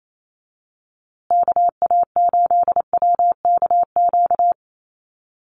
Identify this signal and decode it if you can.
Morse code, heard as a keyed tone, about 28 wpm, 700 Hz. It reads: KA8WKQ
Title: USA Call Signs Repeated 2x 28wpm